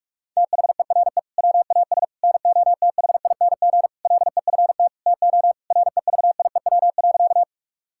Morse code, heard as a keyed tone, about 46 wpm, 700 Hz. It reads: THERE WAS NOTHING LEFT TO REVIEW.
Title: Sentences from Top 1000 Words - Review Part 6 - 45wpm